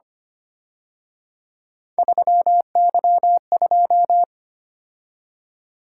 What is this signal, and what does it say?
Morse code, heard as a keyed tone, about 29 wpm, 700 Hz. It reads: 3Y2
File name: Speed-Racing - Three Letter-Number Combinations - 25wpm